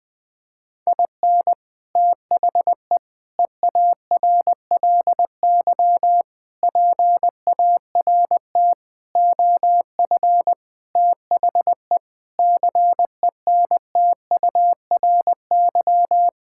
Morse code, heard as a keyed tone, about 21 wpm, 700 Hz. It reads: IN THE EARLY PART OF THE CENTURY
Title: Sentences with 6-Word Phrases - 20wpm